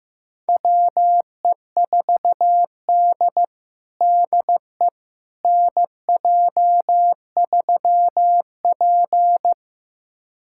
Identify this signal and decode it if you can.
Morse code, heard as a keyed tone, about 15 wpm, 700 Hz. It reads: WE4D DE NJ3P
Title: USA Callsign DE Callsign - 4 Characters - 15wpm